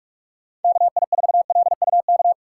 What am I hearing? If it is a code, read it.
Morse code, heard as a keyed tone, about 41 wpm, 700 Hz. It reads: KI4LUK
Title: USA Call Signs - 3x Character Spacing - 45wpm